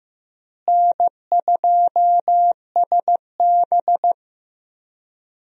Morse code, heard as a keyed tone, about 15 wpm, 700 Hz. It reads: N2SB
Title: USA Call Signs - 4 Characters - Repeated 2x - 15wpm